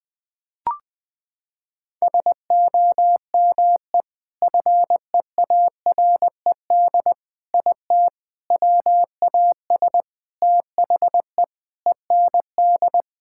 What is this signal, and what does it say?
Morse code, heard as a keyed tone, about 21 wpm, 700 Hz. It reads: SOME FEARED IT WAS THE END
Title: Sentences with 5-Word Phrases - 20wpm